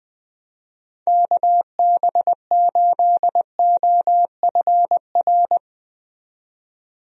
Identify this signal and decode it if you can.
Morse code, heard as a keyed tone, about 20 wpm, 700 Hz. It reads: KB8OFR
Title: USA Call Signs - 6 Characters - 20wpm